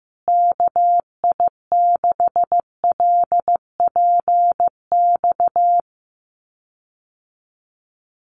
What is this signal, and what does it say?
Morse code, heard as a keyed tone, about 15 wpm, 700 Hz. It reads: KI6LPX
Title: USA Call Signs Repeated 3x 15wpm